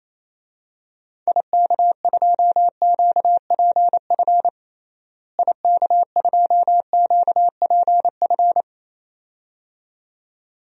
Morse code, heard as a keyed tone, about 28 wpm, 700 Hz. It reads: IK2QPF IK2QPF
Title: DXCC-100 Call Signs - 6 Characters - Repeated 2x - 28wpm